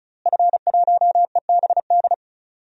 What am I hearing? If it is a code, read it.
Morse code, heard as a keyed tone, about 35 wpm, 700 Hz. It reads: F1EBD
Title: DXCC-100 Call Signs - 5 Characters - Repeated 3x - 35wpm